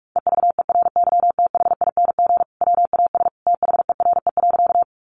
Morse code, heard as a keyed tone, about 44 wpm, 700 Hz. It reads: EVERYTHING WAS THERE.
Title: Sentences from Top 700 Words 45wpm